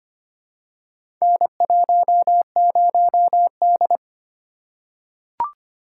Morse code, heard as a keyed tone, about 25 wpm, 700 Hz. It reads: N10D
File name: practice / 25wpm / Four Letter-Number Combinations - 25wpm